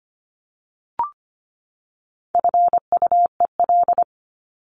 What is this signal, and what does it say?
Morse code, heard as a keyed tone, about 25 wpm, 700 Hz. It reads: FUEL
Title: Top 1751-2000 Words 25wpm